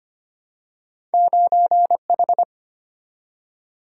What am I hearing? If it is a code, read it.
Morse code, heard as a keyed tone, about 25 wpm, 700 Hz. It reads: 9H
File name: Two Letter-Number Combinations - 25wpm